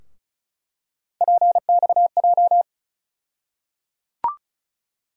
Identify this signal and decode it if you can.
Morse code, heard as a keyed tone, about 36 wpm, 700 Hz. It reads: PXJ
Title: USA Call Sign Postfixes - 3 Letters 35wpm